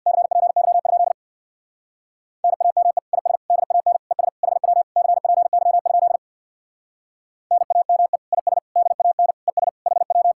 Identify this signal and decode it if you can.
Morse code, heard as a keyed tone, about 63 wpm, 700 Hz. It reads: CPY? NAME IS DAN ES HW CPY? NAME IS DAN ES HW